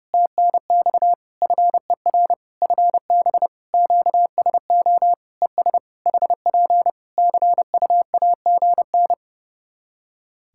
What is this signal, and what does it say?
Morse code, heard as a keyed tone, about 30 wpm, 700 Hz. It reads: TNX FER FB QSO ES HP CUAGN
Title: K1USN SST QSO Practice 30wpm